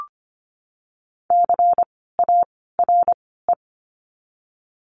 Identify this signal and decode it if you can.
Morse code, heard as a keyed tone, about 25 wpm, 700 Hz. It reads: C A R E
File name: course / 25:10wpm / 094 Lesson 16f - Letter c - Words 25wpm